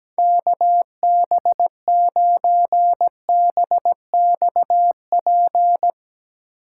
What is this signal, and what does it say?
Morse code, heard as a keyed tone, about 17 wpm, 700 Hz. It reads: KB9BXP
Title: USA Call Signs - 4x Character Spacing - 17wpm